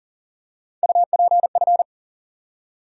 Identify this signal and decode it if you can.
Morse code, heard as a keyed tone, about 40 wpm, 700 Hz. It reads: UPF
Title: USA Call Sign Postfixes - 3 Letters 40wpm